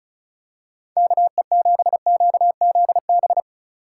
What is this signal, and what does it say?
Morse code, heard as a keyed tone, about 36 wpm, 700 Hz. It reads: KE7QZB
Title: USA Call Signs - 6 Characters - 3x Character Spacing - 35wpm